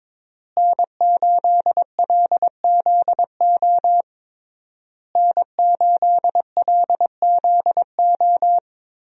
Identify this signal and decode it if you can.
Morse code, heard as a keyed tone, about 22 wpm, 700 Hz. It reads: N8LZO N8LZO
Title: USA Call Signs - 5 Characters - Repeated 3x - 22wpm